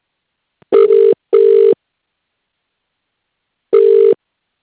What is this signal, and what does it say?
Ringback tone (UK), 400 + 450 Hz.